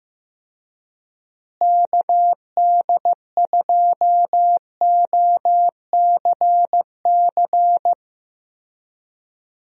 Morse code, heard as a keyed tone, about 15 wpm, 700 Hz. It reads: KD2OCC
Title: USA Call Signs - 6 Characters - Repeated 2x - 15wpm